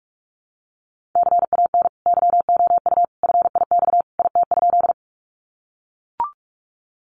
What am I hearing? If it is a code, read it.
Morse code, heard as a keyed tone, about 44 wpm, 700 Hz. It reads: CAN YOU FIX IT?
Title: Sentences from Top 1400 Words 45wpm